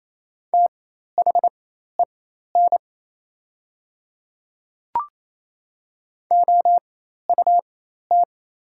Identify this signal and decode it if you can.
Morse code, heard as a keyed tone, about 28 wpm, spaced as for 13 wpm, 700 Hz. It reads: THEN OUT
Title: 4x Character Spacing - Top 100 Words - 28wpm